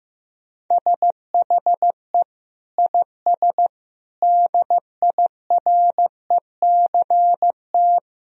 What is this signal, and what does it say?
Morse code, heard as a keyed tone, about 16 wpm, 700 Hz. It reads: SHE IS DIRECT
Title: Sentences from Top 1000 Words - Review Part 4 - 15wpm